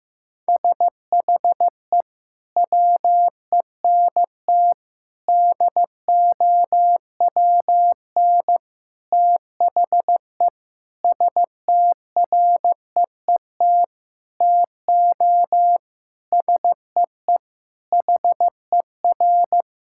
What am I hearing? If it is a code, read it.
Morse code, heard as a keyed tone, about 15 wpm, 700 Hz. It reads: SHE WENT DOWN THE STREET TO SEE HER
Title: Sentences from Top 1000 Words - Review Part 5 - 15wpm